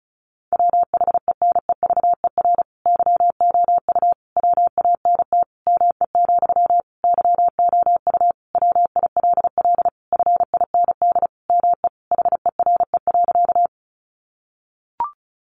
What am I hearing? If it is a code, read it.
Morse code, heard as a keyed tone, about 34 wpm, 700 Hz. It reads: WHENEVER YOU WANT ME, YOU WILL FIND ME HERE.
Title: Sentences from Top 1200 Words 35wpm